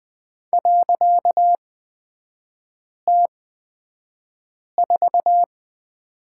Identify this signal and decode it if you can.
Morse code, heard as a keyed tone, about 20 wpm, spaced as for 5 wpm, 700 Hz. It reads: .T4
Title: 255 Lesson 41e - Period - Three Characters 20wpm